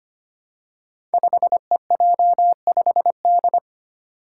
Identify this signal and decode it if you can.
Morse code, heard as a keyed tone, about 25 wpm, 700 Hz. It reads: 5EJ5D
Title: Five Challenging Letter-Number Combinations 25wpm